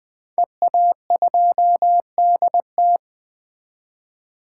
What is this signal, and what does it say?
Morse code, heard as a keyed tone, about 20 wpm, 700 Hz. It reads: EA2DT
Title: 2023 SOTA Chaser Call Signs - 20wpm